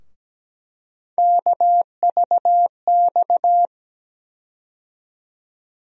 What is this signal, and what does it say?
Morse code, heard as a keyed tone, about 17 wpm, 700 Hz. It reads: KVX
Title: USA Call Sign Postfixes - 3 Letters 17wpm